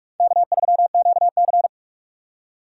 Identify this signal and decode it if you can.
Morse code, heard as a keyed tone, about 46 wpm, 700 Hz. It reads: K3QC
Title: USA Call Signs - 4 Characters - 4x Character Spacing - 45wpm